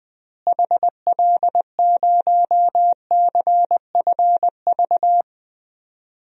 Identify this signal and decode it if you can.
Morse code, heard as a keyed tone, about 20 wpm, 700 Hz. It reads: HL0CFV
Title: DXCC-100 Call Signs - 6 Characters - Repeated 2x - 20wpm